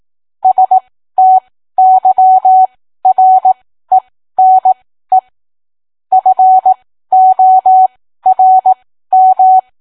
Morse code, heard as a keyed tone, about 18 wpm, spaced as for 13 wpm, 750 Hz. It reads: STYRENE FORM